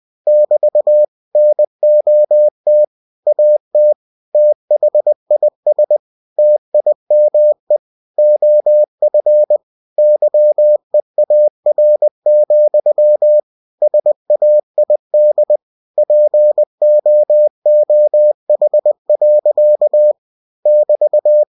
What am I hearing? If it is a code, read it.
Morse code, heard as a keyed tone, about 20 wpm, 600 Hz. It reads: <BT> NOT AT THIS TIME OF YEAR, SAID POOH. <BT>
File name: cw-books / winnie-the-pooh / winnie-20wpm-page-0033